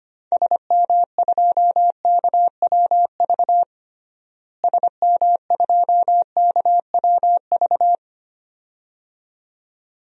Morse code, heard as a keyed tone, about 25 wpm, 700 Hz. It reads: SM2KWV SM2KWV
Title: DXCC-100 Call Signs - 6 Characters - Repeated 2x - 25wpm